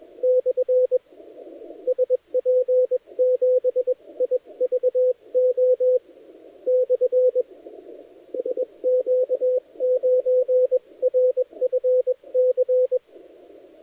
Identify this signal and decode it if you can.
Morse code, heard as a keyed tone, about 21 wpm, spaced as for 19 wpm, 510 Hz, with radio noise. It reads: / SP7IVO / SQ9RFC
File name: dlazaawansowanych / nagraniazpasma / titawaNETcontrol